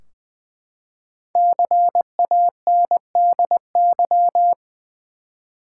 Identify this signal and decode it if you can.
Morse code, heard as a keyed tone, about 20 wpm, 700 Hz. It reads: CANDY
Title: Five Letter Words - 20wpm